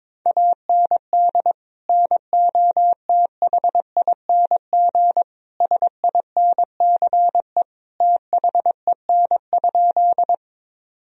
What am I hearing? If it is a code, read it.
Morse code, heard as a keyed tone, about 22 wpm, 700 Hz. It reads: AND NOTHING SINCE THEN?
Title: Sentences from Top 1000 Words - Review Part 1 - 22wpm